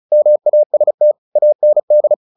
Morse code, heard as a keyed tone, about 34 wpm, 600 Hz. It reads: MAST AND
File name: the old man and the sea 2 35wpm